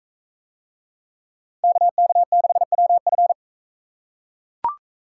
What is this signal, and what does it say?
Morse code, heard as a keyed tone, about 40 wpm, 700 Hz. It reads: KK6WF